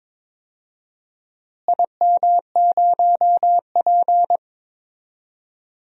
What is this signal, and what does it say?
Morse code, heard as a keyed tone, about 22 wpm, 700 Hz. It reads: IM0P
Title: DXCC-100 Call Signs - 4 Characters - Repeated 3x - 22wpm